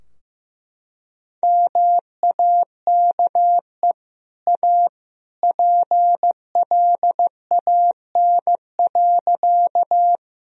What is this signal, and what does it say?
Morse code, heard as a keyed tone, about 15 wpm, 700 Hz. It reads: MAKE A PLAN.
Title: Sentences from Top 1000 Words - Review Part 2 - 15wpm